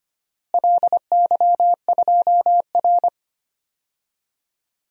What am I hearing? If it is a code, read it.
Morse code, heard as a keyed tone, about 25 wpm, 700 Hz. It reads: LY2R
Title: DXCC-100 Call Signs - 4 Characters - Repeated 2x - 25wpm